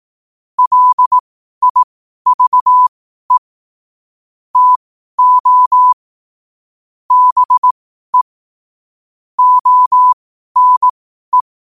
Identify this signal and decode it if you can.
Morse code, heard as a keyed tone, about 18 wpm, spaced as for 12 wpm, 1000 Hz. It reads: LIVE TO BE ONE
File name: qotdfiles / 12wpm